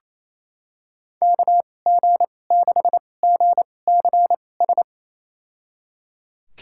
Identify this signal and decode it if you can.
Morse code, heard as a keyed tone, about 27 wpm, spaced as for 20 wpm, 700 Hz. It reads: KG6GCS
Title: USA Call Signs - 6 Characters - 2x Character Spacing - 28wpm